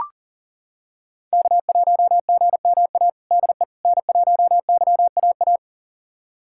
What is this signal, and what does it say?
Morse code, heard as a keyed tone, about 40 wpm, 700 Hz. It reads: K1GMA DE N1YAA
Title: USA Callsign DE Callsign - 5 Characters - 40wpm